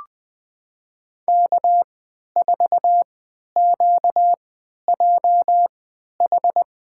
Morse code, heard as a keyed tone, about 20 wpm, spaced as for 11 wpm, 700 Hz. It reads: K4QJH
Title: USA Call Signs - 5 Characters - 3x Character Spacing - 20wpm